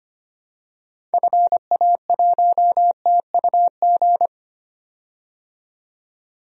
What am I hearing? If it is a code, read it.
Morse code, heard as a keyed tone, about 25 wpm, 700 Hz. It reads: FA1TUG
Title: DXCC-100 Call Signs - 6 Characters - 25wpm